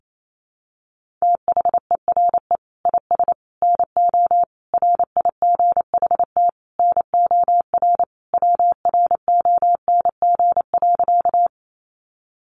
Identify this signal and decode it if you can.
Morse code, heard as a keyed tone, about 28 wpm, 700 Hz. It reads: THERE IS NO RIGHT NOR WRONG.
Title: Sentences from Top 1500 Words 28wpm